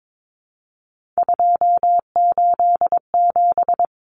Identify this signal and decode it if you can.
Morse code, heard as a keyed tone, about 22 wpm, 700 Hz. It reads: 287
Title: Numbers - 3-Digit 22wpm